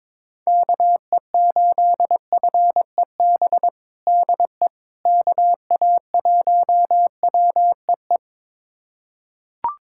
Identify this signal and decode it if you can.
Morse code, heard as a keyed tone, about 22 wpm, 700 Hz. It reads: KE8FEB DE KA1WEE
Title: USA Callsign DE Callsign - 6 Characters - 22wpm